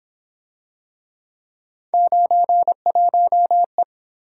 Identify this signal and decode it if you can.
Morse code, heard as a keyed tone, about 27 wpm, 700 Hz. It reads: 91E
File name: Speed-Racing - Three Letter-Number Combinations - 22wpm